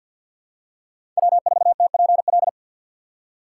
Morse code, heard as a keyed tone, about 52 wpm, 690 Hz. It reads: W4TPL
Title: USA Call Signs - 5 Characters - 2x Character Spacing - 50wpm